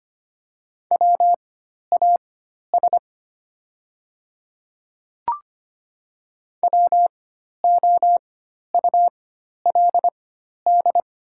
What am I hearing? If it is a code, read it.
Morse code, heard as a keyed tone, about 25 wpm, spaced as for 12 wpm, 700 Hz. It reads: WAS WOULD
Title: 4x Character Spacing - Top 300 Words - 25wpm